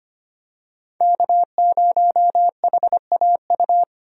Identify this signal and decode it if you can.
Morse code, heard as a keyed tone, about 25 wpm, 700 Hz. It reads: K0HAU
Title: USA Call Signs - 5 Characters - 25wpm